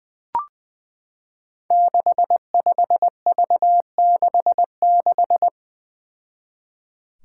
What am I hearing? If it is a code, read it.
Morse code, heard as a keyed tone, about 20 wpm, 700 Hz. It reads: 65V66
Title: Five Challenging Letter-Number Combinations 20wpm